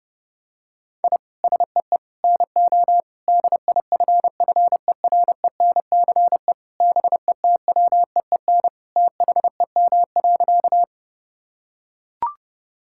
Morse code, heard as a keyed tone, about 32 wpm, 700 Hz. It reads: I SEE NO DIFFERENCE BETWEEN THEM.
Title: Sentences from Top 1000 Words - Review Part 5 - 30wpm